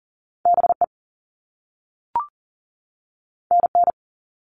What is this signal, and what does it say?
Morse code, heard as a keyed tone, about 38 wpm, 700 Hz. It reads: BE NN
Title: All Cut Numbers - 2-Digit 40wpm